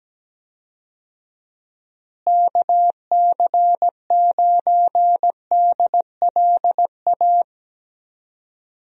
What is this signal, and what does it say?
Morse code, heard as a keyed tone, about 17 wpm, 700 Hz. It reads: KC9DLA
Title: USA Call Signs - 6 Characters - Repeated 3x - 17wpm